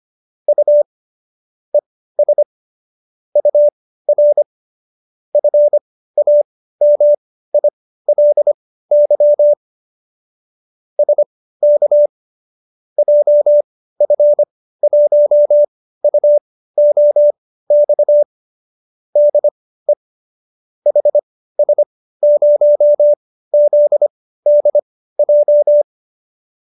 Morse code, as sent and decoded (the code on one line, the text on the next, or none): ..- / . ... / ..- .-. / ..-. .- -- .. .-.. -.-- / ... -.- / .--- ..-. .---- ..- --- -..- / -.. / .... ... ----- --.. -.. .---
U ES UR FAMILY SK JF1UOX DE HS0ZDJ